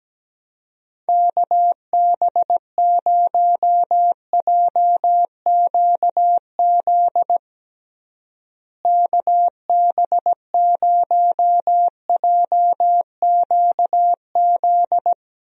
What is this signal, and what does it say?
Morse code, heard as a keyed tone, about 17 wpm, 700 Hz. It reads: KB0JQZ KB0JQZ
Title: USA Call Signs - 6 Characters - Repeated 3x - 17wpm